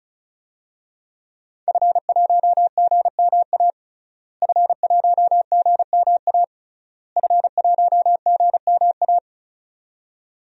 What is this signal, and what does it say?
Morse code, heard as a keyed tone, about 36 wpm, 700 Hz. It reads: F1GMA F1GMA F1GMA
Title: DXCC-100 Call Signs - 5 Characters - Repeated 3x - 35wpm